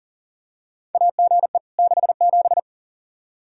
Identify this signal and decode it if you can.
Morse code, heard as a keyed tone, about 40 wpm, 700 Hz. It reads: AGE 67
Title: K1USN SST QSO Practice 40wpm